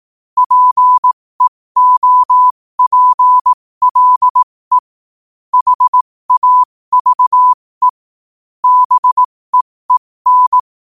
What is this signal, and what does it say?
Morse code, heard as a keyed tone, about 18 wpm, spaced as for 15 wpm, 1000 Hz. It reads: PEOPLE HAVE BEEN
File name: qotdfiles / 15wpm